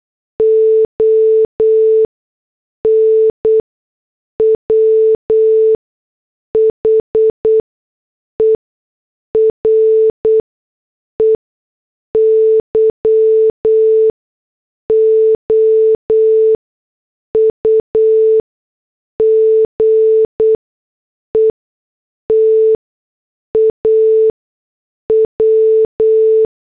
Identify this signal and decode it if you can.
Morse code, heard as a keyed tone, about 8 wpm, spaced as for 6 wpm, 440 Hz. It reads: ON WHERE YOU GET A W
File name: morse / headlines